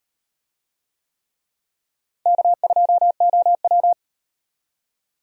Morse code, heard as a keyed tone, about 35 wpm, 700 Hz. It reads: K2OW